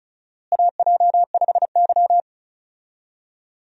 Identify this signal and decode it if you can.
Morse code, heard as a keyed tone, about 36 wpm, 700 Hz. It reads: AJ5Y
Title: USA Call Signs - 4 Characters - Repeated 3x - 35wpm